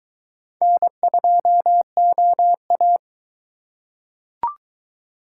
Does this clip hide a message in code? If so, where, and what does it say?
Yes: Morse code from 0.6 to 3.0 s, heard as a keyed tone, about 22 wpm, 700 Hz. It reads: N2OA